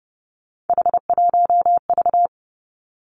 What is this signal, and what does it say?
Morse code, heard as a keyed tone, about 30 wpm, 700 Hz. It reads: H1V
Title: Three Challenging Letter-Number Combinations 30wpm